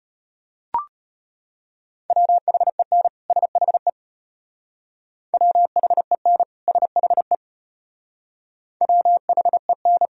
Morse code, heard as a keyed tone, about 33 wpm, 700 Hz. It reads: WHEN SHE WHEN SHE WHEN